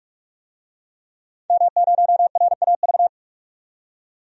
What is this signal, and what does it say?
Morse code, heard as a keyed tone, about 44 wpm, 700 Hz. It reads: M0RAV
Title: DXCC-100 Call Signs - 5 Characters - 45wpm